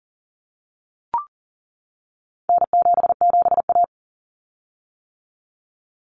Morse code, heard as a keyed tone, about 40 wpm, 700 Hz. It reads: N77A
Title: Common Cut Numbers - 4-Digit 40wpm